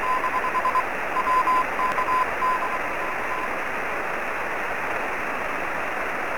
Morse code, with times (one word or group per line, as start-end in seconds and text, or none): none